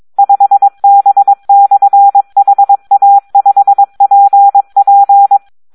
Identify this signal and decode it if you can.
Morse code, heard as a keyed tone, about 22 wpm, 800 Hz. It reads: 5B/HA5PP
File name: Audio practica CW / SD 1 22